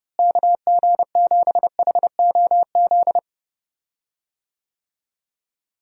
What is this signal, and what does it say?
Morse code, heard as a keyed tone, about 30 wpm, 700 Hz. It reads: KG7HOZ
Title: USA Call Signs - 6 Characters - Repeated 3x - 30wpm